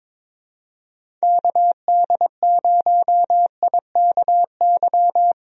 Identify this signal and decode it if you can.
Morse code, heard as a keyed tone, about 22 wpm, 700 Hz. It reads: KD0IKY
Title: USA Call Signs - 6 Characters - Repeated 3x - 22wpm